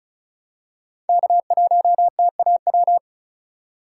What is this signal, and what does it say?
Morse code, heard as a keyed tone, about 34 wpm, 700 Hz. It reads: K1TAW
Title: USA Call Signs - 3x Character Spacing - 35wpm